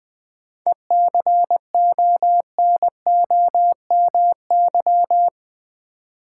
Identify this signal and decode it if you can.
Morse code, heard as a keyed tone, about 20 wpm, 700 Hz. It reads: ECONOMY
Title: Sentences from Top 1000 Words - Review Part 3 - New Word Drill - 20wpm